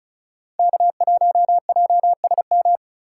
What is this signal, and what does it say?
Morse code, heard as a keyed tone, about 36 wpm, 700 Hz. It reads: K1JSM
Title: USA Call Signs - 3x Character Spacing - 35wpm